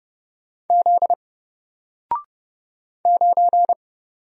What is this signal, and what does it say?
Morse code, heard as a keyed tone, about 29 wpm, 700 Hz. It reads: Z 9
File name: Single-Letter-Number-30wpm